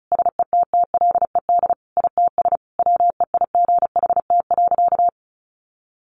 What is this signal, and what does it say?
Morse code, heard as a keyed tone, about 34 wpm, 700 Hz. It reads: SETTLED ITS WEIGHT.
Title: A Sound Of Thunder 35wpm